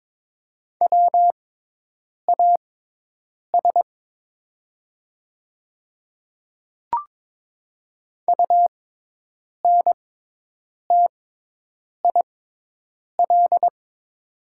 Morse code, heard as a keyed tone, about 22 wpm, spaced as for 8 wpm, 700 Hz. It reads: WAS UNTIL
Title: 6x Character Spacing - Top 300 Words - 22wpm